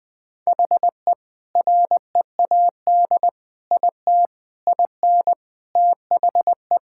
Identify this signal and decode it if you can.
Morse code, heard as a keyed tone, about 20 wpm, 700 Hz. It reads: HE READ IT IN THE
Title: Sentences with 4-Word Phrases - 20wpm